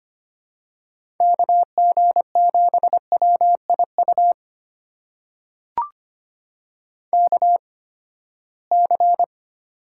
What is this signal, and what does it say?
Morse code, heard as a keyed tone, about 25 wpm, spaced as for 17 wpm, 700 Hz. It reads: KG7WIU KC
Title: USA Call Signs - 6 Characters - 8x Character Spacing - 25wpm